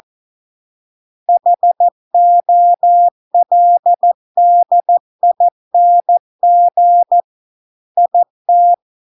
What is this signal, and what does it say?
Morse code, heard as a keyed tone, about 14 wpm, 700 Hz. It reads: HOLDING IT